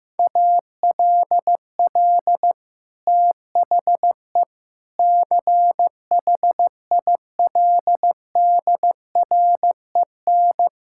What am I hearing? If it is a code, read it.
Morse code, heard as a keyed tone, about 15 wpm, 700 Hz. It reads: ALL THE CHILDREN
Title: Sentences from Top 1000 Words - Review Part 6 - 15wpm